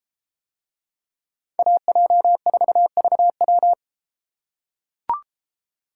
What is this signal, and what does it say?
Morse code, heard as a keyed tone, about 33 wpm, 700 Hz. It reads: AJ4VW